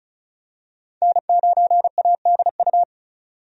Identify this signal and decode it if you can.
Morse code, heard as a keyed tone, about 36 wpm, 700 Hz. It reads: N9ADU
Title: USA Call Signs - 5 Characters - 35wpm